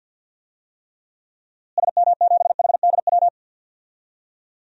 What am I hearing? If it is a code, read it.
Morse code, heard as a keyed tone, about 47 wpm, 700 Hz. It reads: SM7HDW
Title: DXCC-100 Call Signs - 6 Characters - Repeated 2x - 50wpm